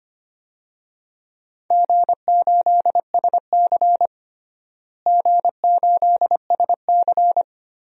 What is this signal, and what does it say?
Morse code, heard as a keyed tone, about 25 wpm, 700 Hz. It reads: G8SC G8SC
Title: DXCC-100 Call Signs - 4 Characters - Repeated 3x - 25wpm